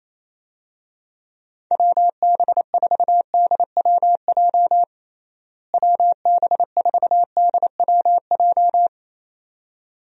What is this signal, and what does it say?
Morse code, heard as a keyed tone, about 27 wpm, 700 Hz. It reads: WB4DWJ WB4DWJ
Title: USA Call Signs - 6 Characters - Repeated 2x - 28wpm